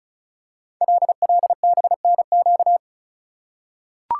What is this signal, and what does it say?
Morse code, heard as a keyed tone, about 36 wpm, 690 Hz. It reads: LLBNQ
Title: Code Groups - 5 Letters - 35wpm